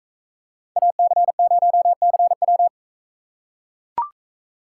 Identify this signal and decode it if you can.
Morse code, heard as a keyed tone, about 42 wpm, 700 Hz. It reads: AC0CW